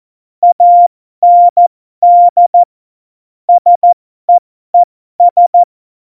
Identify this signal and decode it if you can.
Morse code, heard as a keyed tone, about 14 wpm, spaced as for 12 wpm, 700 Hz. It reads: AND SEES
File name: Morsekurs / Witze